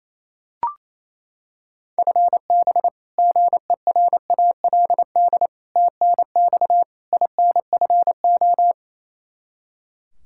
Morse code, heard as a keyed tone, about 28 wpm, 700 Hz. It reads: FB GERALD TNX INFO
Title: Practice QSO - Exchange 3 - More Info - 28wpm